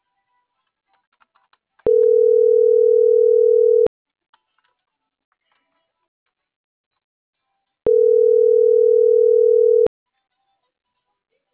Ringback tone (North America), 440 + 480 Hz.